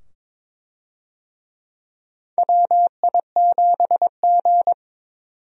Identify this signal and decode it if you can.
Morse code, heard as a keyed tone, about 22 wpm, 700 Hz. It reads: WI7G
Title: USA Call Signs - 4 Characters - 22wpm